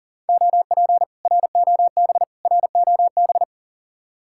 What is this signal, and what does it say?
Morse code, heard as a keyed tone, about 51 wpm, 700 Hz. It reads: OP ROB ROB